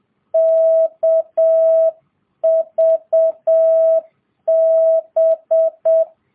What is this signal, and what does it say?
Morse code, heard as a keyed tone, about 7 wpm, 650 Hz. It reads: KVB